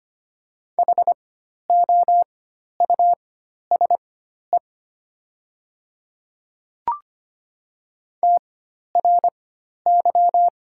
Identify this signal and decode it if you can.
Morse code, heard as a keyed tone, about 25 wpm, spaced as for 12 wpm, 690 Hz. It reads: HOUSE TRY
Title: 4x Character Spacing - Top 200 Words - 25wpm